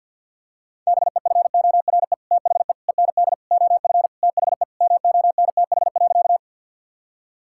Morse code, heard as a keyed tone, about 50 wpm, 700 Hz. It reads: BEFORE THE END OF THE MONTH.
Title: Sentences from Top 1000 Words - Review Part 5 - 50wpm